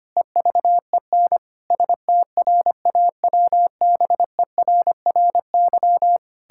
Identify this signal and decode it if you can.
Morse code, heard as a keyed tone, about 25 wpm, 700 Hz. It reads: EVEN STRAWBERRY
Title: Pangram Challenge v2 - 25wpm